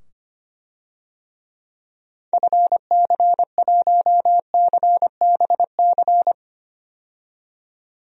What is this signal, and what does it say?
Morse code, heard as a keyed tone, about 25 wpm, 700 Hz. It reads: FC1CBC
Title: DXCC-100 Call Signs - 6 Characters - 25wpm